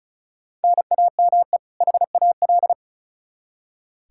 Morse code, heard as a keyed tone, about 36 wpm, 700 Hz. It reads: NAME HAL
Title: Practice QSO - Exchange 1 - Beginning - 35wpm